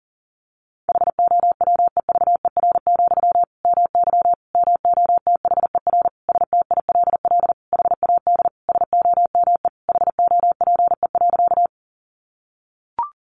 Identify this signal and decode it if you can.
Morse code, heard as a keyed tone, about 40 wpm, 700 Hz. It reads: HOWEVER, MY MOTHER STILL HAD SOME HOPE.
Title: Sentences from Top 1000 Words 40wpm